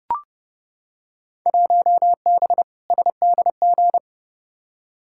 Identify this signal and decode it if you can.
Morse code, heard as a keyed tone, about 30 wpm, 700 Hz. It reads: 1B SDG
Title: Field Day Exchanges - 30wpm